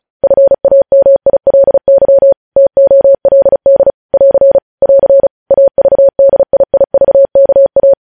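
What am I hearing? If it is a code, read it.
Morse code, heard as a keyed tone, about 36 wpm, 550 Hz. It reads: FAMILY TOLD <AR> <AR> AVDIIVKA
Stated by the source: Morse news headlines practice at 35.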